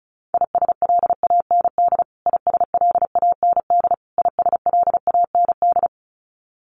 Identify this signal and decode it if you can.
Morse code, heard as a keyed tone, about 34 wpm, 700 Hz. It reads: ISLAND ISLAND ISLAND
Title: Top 500 Words Repeated 35wpm